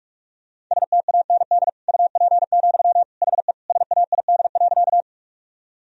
Morse code, heard as a keyed tone, about 46 wpm, 700 Hz. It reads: STAND UP, HE SAID.
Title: Sentences from Top 1000 Words - Review Part 1 - 45wpm